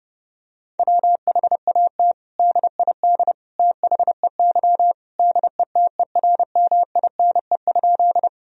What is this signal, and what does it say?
Morse code, heard as a keyed tone, about 30 wpm, 700 Hz. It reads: WHAT DID THEY DETERMINE?
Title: Sentences from Top 1000 Words - Review Part 4 - 30wpm